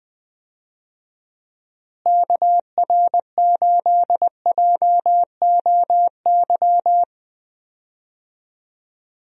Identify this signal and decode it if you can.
Morse code, heard as a keyed tone, about 20 wpm, 700 Hz. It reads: KR8JOY
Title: USA Call Signs - 6 Characters - Repeated 3x - 20wpm